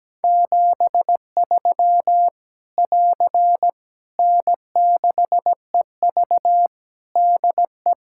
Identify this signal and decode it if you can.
Morse code, heard as a keyed tone, about 17 wpm, 700 Hz. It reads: 73 <AR> N6EV DE
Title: K1USN SST QSO Practice 17wpm